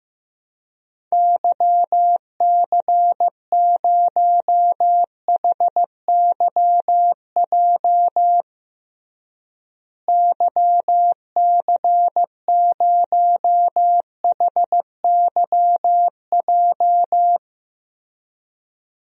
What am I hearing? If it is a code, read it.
Morse code, heard as a keyed tone, about 15 wpm, 700 Hz. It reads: YC0HYJ YC0HYJ
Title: DXCC-100 Call Signs - 6 Characters - Repeated 2x - 15wpm